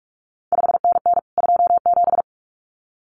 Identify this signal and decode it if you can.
Morse code, heard as a keyed tone, about 46 wpm, 700 Hz. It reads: 5NN 27
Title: CQ WW DX Contest Exchange Encore - 45wpm